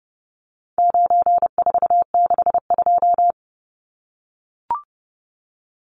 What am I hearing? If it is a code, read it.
Morse code, heard as a keyed tone, about 30 wpm, 700 Hz. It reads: 9462
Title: Numbers - 4-Digit 30wpm